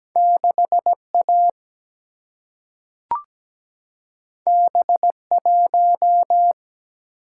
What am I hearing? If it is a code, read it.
Morse code, heard as a keyed tone, about 17 wpm, 700 Hz. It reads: 6 A B 1
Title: Single-Letter Single-Number 17wpm